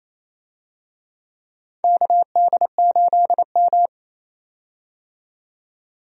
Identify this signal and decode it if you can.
Morse code, heard as a keyed tone, about 28 wpm, 700 Hz. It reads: KD8M
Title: USA Call Signs - 4 Characters - 28wpm